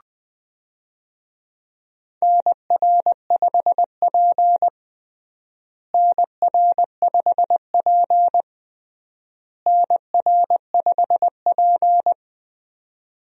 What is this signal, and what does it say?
Morse code, heard as a keyed tone, about 20 wpm, 700 Hz. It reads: NR5P NR5P NR5P
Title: USA Call Signs - 4 Characters - Repeated 3x - 20wpm